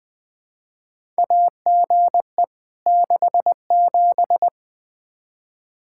Morse code, heard as a keyed tone, about 20 wpm, 700 Hz. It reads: AGE 67
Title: K1USN SST QSO Practice 20wpm